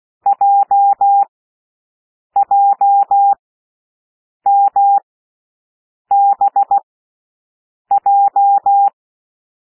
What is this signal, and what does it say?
Morse code, heard as a keyed tone, about 16 wpm, spaced as for 6 wpm, 800 Hz. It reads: JJMBJ
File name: cwmp3 / L1 SOUND LETTERS 16 wpm